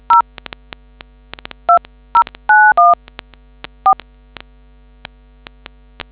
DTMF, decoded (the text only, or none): 0 2091 4